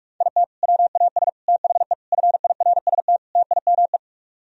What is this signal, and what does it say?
Morse code, heard as a keyed tone, about 44 wpm, 700 Hz. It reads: IT WAS THE FIRST TIME